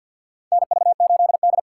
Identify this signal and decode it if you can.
Morse code, heard as a keyed tone, about 47 wpm, 690 Hz. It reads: NV8D
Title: USA Call Signs - 4x Character Spacing - 50wpm